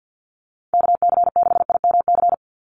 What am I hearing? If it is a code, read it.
Morse code, heard as a keyed tone, about 50 wpm, 700 Hz. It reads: KC6IMC
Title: USA Call Signs 50wpm